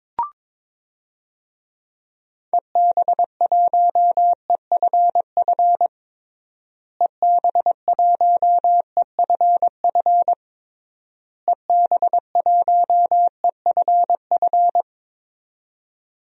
Morse code, heard as a keyed tone, about 22 wpm, 700 Hz. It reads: EB1EFF EB1EFF EB1EFF
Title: DXCC-100 Call Signs - 6 Characters - Repeated 3x - 22wpm